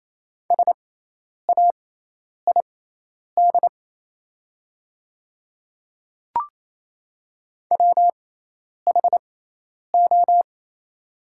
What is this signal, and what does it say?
Morse code, heard as a keyed tone, about 28 wpm, spaced as for 10 wpm, 700 Hz. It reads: SAID WHO
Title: 6x Character Spacing - Top 100 Words - 28wpm